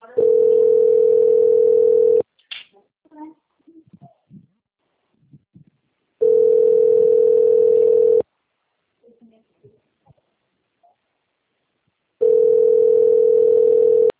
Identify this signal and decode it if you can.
Ringback tone (North America), 440 + 480 Hz, with speech.
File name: out- - - - - .59070